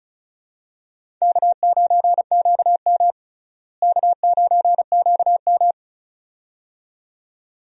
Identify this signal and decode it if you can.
Morse code, heard as a keyed tone, about 34 wpm, 700 Hz. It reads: K9QM K9QM
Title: USA Call Signs - 4 Characters - Repeated 2x - 35wpm